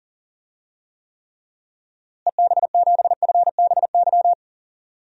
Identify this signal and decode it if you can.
Morse code, heard as a keyed tone, about 40 wpm, 700 Hz. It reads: EB7FBY
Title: DXCC-100 Call Signs - 6 Characters - Repeated 3x - 40wpm